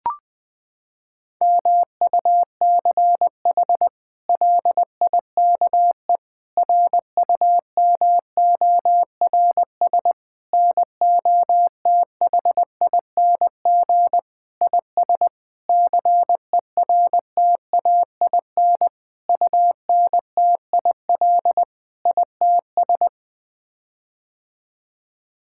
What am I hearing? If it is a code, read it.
Morse code, heard as a keyed tone, about 20 wpm, 700 Hz. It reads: MUCH LIKE RUMORS NOTHING IS CERTAIN UNTIL ITS
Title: Sets of 8 Words - 20wpm